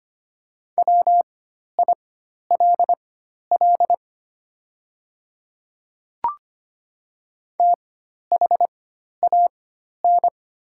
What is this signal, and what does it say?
Morse code, heard as a keyed tone, about 25 wpm, spaced as for 12 wpm, 700 Hz. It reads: WILL THAN
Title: 4x Character Spacing - Top 100 Words - 25wpm